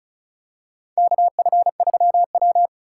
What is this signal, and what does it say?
Morse code, heard as a keyed tone, about 36 wpm, 700 Hz. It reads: KF3W
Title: USA Call Signs - 4 Characters - 4x Character Spacing - 35wpm